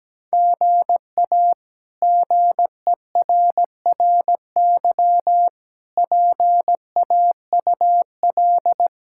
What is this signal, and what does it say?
Morse code, heard as a keyed tone, about 17 wpm, 700 Hz. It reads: GA GERRY PAUL
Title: K1USN SST QSO Practice - 17wpm